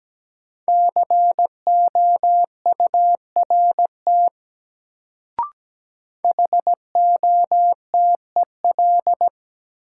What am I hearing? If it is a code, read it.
Morse code, heard as a keyed tone, about 17 wpm, 700 Hz. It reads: COURT HOTEL
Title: Sentences from Top 1600 Words - New Word Drill - 17wpm